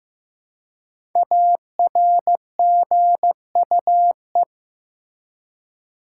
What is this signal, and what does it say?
Morse code, heard as a keyed tone, about 16 wpm, 700 Hz. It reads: ARGUE
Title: Five Letter Words - 15wpm